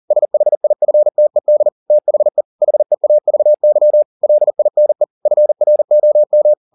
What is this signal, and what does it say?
Morse code, heard as a keyed tone, about 40 wpm, 600 Hz. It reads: SHIFTED THE HEAVY LINE FROM
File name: the old man and the sea 2 40wpm